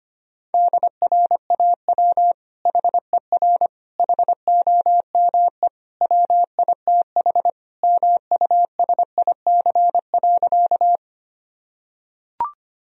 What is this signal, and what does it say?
Morse code, heard as a keyed tone, about 25 wpm, 700 Hz. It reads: DRAW HER HOME WITH MUSIC.
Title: Sentences from Top 1000 Words - Review Part 1 - 25wpm